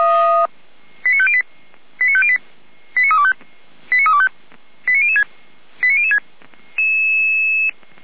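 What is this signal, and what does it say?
Five-tone selective call (ZVEI), heard as quick runs of five tones, tones of about 70 ms. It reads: 89508 89508 89325 89325 89006 89006
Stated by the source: ZVEI five-tone selective call 89014 received by SDR.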